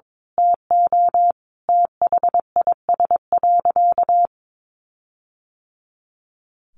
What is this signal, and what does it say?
Morse code, heard as a keyed tone, about 22 wpm, 700 Hz. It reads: TO THIS.
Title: Sentences from Top 1400 Words 22wpm